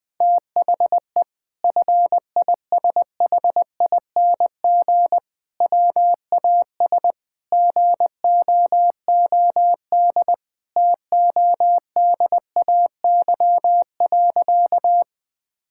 Morse code, heard as a keyed tone, about 20 wpm, 700 Hz. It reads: THE FISHING WAS GOOD TODAY.
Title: Sentences from Top 1000 Words - Review Part 5 - 20wpm